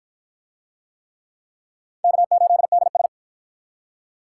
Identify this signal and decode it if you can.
Morse code, heard as a keyed tone, about 48 wpm, 700 Hz. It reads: K8DS